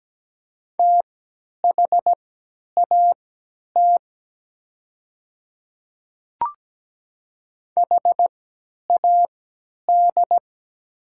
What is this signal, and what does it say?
Morse code, heard as a keyed tone, about 17 wpm, spaced as for 10 wpm, 690 Hz. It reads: THAT HAD
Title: 3x Character Spacing - Top 100 Words - 17wpm